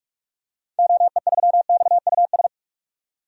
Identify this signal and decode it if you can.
Morse code, heard as a keyed tone, about 44 wpm, 700 Hz. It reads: OE3XUS
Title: DXCC-100 Call Signs - 6 Characters - 45wpm